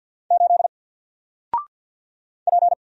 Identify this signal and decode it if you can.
Morse code, heard as a keyed tone, about 48 wpm, 700 Hz. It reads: 8 P
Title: Single Letter-Number 50wpm